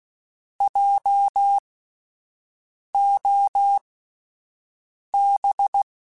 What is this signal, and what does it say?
Morse code, heard as a keyed tone, about 16 wpm, spaced as for 5 wpm, 800 Hz. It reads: JOB
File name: Spaced Alphas / Spaced Alphas mp3s / Js sp @10